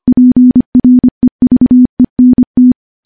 Morse code, heard as a keyed tone, about 25 wpm, 250 Hz. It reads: PREVENT
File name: morse / morse-training-25WPM